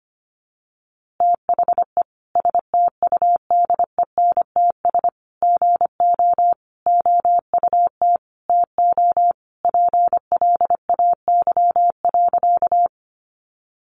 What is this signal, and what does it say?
Morse code, heard as a keyed tone, about 25 wpm, 700 Hz. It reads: THE STUDENTS GO OUT TO PLAY.
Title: Easy Phrases 25wpm